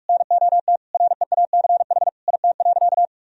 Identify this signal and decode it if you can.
Morse code, heard as a keyed tone, about 45 wpm, 700 Hz. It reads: NOT REACH IT.
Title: Sentences from Top 1000 Words - Review Part 2 - 45wpm